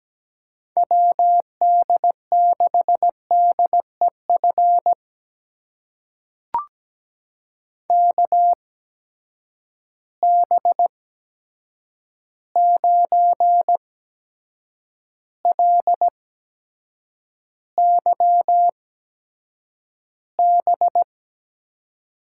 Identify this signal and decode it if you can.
Morse code, heard as a keyed tone, about 17 wpm, spaced as for 7 wpm, 700 Hz. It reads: WD6DEF KB9LYB
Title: USA Call Signs - 6 Characters - 8x Character Spacing - 17wpm